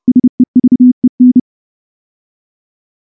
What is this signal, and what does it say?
Morse code, heard as a keyed tone, about 28 wpm, 250 Hz. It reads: SEVEN
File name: morse / morse-training-30WPM